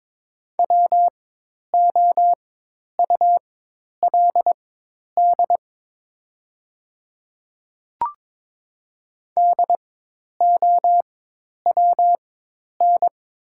Morse code, heard as a keyed tone, about 22 wpm, spaced as for 10 wpm, 700 Hz. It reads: WOULD DOWN
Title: 4x Character Spacing - Top 300 Words - 22wpm